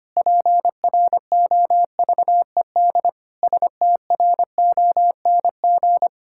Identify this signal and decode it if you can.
Morse code, heard as a keyed tone, about 25 wpm, 700 Hz. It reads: PROVED STRONG
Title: Pangram Challenge v2 - 25wpm